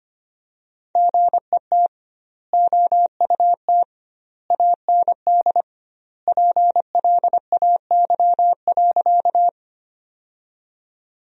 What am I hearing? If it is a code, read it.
Morse code, heard as a keyed tone, about 25 wpm, 700 Hz. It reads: GET OUT AND PLAY.
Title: 2x Word-Spacing - Sentences from Top 200 Words - 25wpm